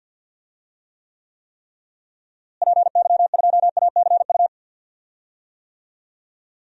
Morse code, heard as a keyed tone, about 51 wpm, 700 Hz. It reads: PY2ACU
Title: DXCC-100 Call Signs - 6 Characters - Repeated 3x - 50wpm